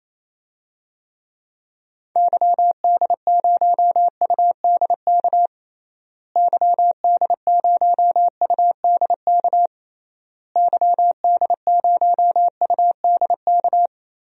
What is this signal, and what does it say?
Morse code, heard as a keyed tone, about 28 wpm, 700 Hz. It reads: YD0UDK YD0UDK YD0UDK
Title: DXCC-100 Call Signs - 6 Characters - Repeated 3x - 28wpm